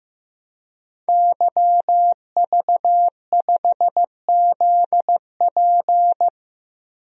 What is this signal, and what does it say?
Morse code, heard as a keyed tone, about 15 wpm, 700 Hz. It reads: YV5ZP
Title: DXCC-100 Call Signs - 5 Characters - 15wpm